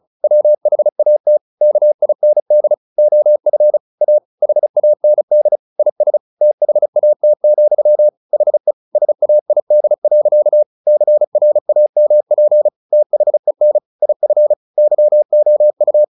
Morse code, heard as a keyed tone, about 35 wpm, 600 Hz. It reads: WHAT KIND OF A HAND IS THAT, HE SAID. CRAMP THEN IF YOU
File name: the old man and the sea / the old man and the sea 1 35wpm